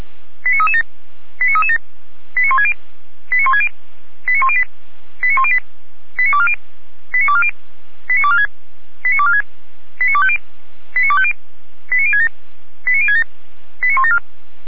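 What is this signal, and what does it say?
Five-tone selective call (ZVEI), heard as quick runs of five tones, tones of about 70 ms. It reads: 89337 89337 89160 89160 89108 89108 89240 89240 89246 89246 89250 89250 89067 89067 89174
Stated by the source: ZVEI five-tone selective call received on SDR.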